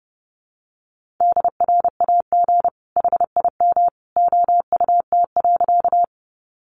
Morse code, heard as a keyed tone, about 30 wpm, 700 Hz. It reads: DRAG HIM OUT.
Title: Sentences from Top 1500 Words 30wpm